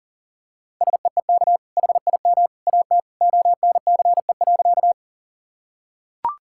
Morse code, heard as a keyed tone, about 40 wpm, 700 Hz. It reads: SEEK HIM AT ONCE.
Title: Sentences from Top 1600 Words 40wpm